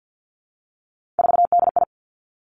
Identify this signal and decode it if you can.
Morse code, heard as a keyed tone, about 50 wpm, 700 Hz. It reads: 4DI
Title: Three Challenging Letter-Number Combinations 50wpm